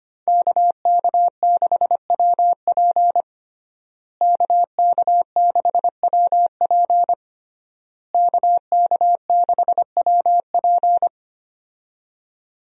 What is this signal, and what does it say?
Morse code, heard as a keyed tone, about 25 wpm, 700 Hz. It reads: KK6WP KK6WP KK6WP
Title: USA Call Signs - 5 Characters - Repeated 3x - 25wpm